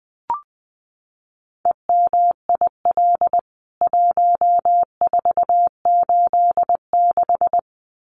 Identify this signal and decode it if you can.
Morse code, heard as a keyed tone, about 20 wpm, 700 Hz. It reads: EMIL 1486
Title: CWT Exchange 20wpm